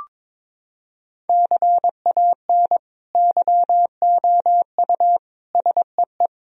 Morse code, heard as a keyed tone, about 21 wpm, 700 Hz. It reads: CAN YOU SEE
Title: Sentences from Top 1000 Words - Review Part 1 - 22wpm